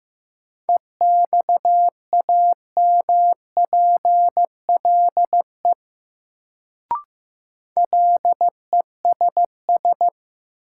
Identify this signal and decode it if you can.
Morse code, heard as a keyed tone, about 15 wpm, 700 Hz. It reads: EXAMPLE LESS
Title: Sentences from Top 400 Words - New Word Drill - 15wpm